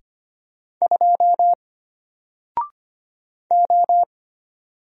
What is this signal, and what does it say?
Morse code, heard as a keyed tone, about 25 wpm, 700 Hz. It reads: 2 O
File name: Single-Letter-Number-25wpm